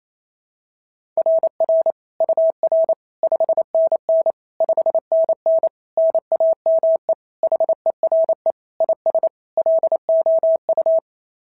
Morse code, heard as a keyed tone, about 25 wpm, 650 Hz. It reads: RR UR 5NN 5NN NAME HERE IS LOU
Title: SOTA QSO Practice 28wpm